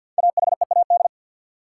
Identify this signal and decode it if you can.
Morse code, heard as a keyed tone, about 53 wpm, 700 Hz. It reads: AHEAD
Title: Sentences from Top 1100 Words - New Word Drill Alt - 50wpm